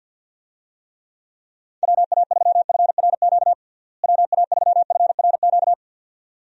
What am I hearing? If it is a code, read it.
Morse code, heard as a keyed tone, about 48 wpm, 700 Hz. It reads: WA3FRQ WA3FRQ
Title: USA Call Signs - 6 Characters - Repeated 2x - 50wpm